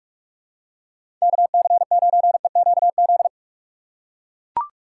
Morse code, heard as a keyed tone, about 42 wpm, 700 Hz. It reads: KC9EQZ
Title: USA Call Signs - 3x Character Spacing - 45wpm